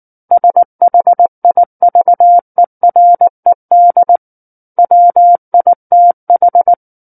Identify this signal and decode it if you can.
Morse code, heard as a keyed tone, about 19 wpm, 700 Hz. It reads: SHIVERED WITH
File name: Book One / 10-Chapter 10 - 19 WPM